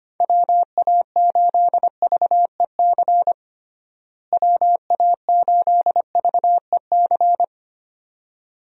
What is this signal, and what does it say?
Morse code, heard as a keyed tone, about 25 wpm, 700 Hz. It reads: WA8VEC WA8VEC
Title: USA Call Signs - 6 Characters - Repeated 2x - 25wpm